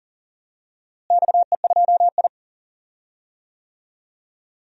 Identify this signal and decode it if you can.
Morse code, heard as a keyed tone, about 40 wpm, 700 Hz. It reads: XE2I
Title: DXCC-100 Call Signs - 4 Characters - Repeated 2x - 40wpm